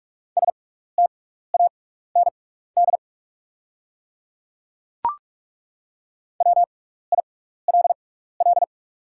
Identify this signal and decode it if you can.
Morse code, heard as a keyed tone, about 42 wpm, spaced as for 15 wpm, 690 Hz. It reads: STAND WILL
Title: 6x Character Spacing - Top 300 Words - 45wpm